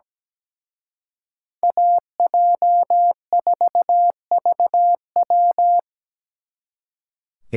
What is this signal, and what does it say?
Morse code, heard as a keyed tone, about 20 wpm, 700 Hz. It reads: AJ4VW
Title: Speed Racing - USA 2x2 Call Signs 17wpm